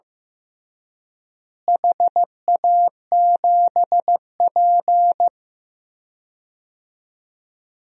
Morse code, heard as a keyed tone, about 15 wpm, 700 Hz. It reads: HA7P
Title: DXCC-100 Call Signs - 4 Characters - Repeated 3x - 15wpm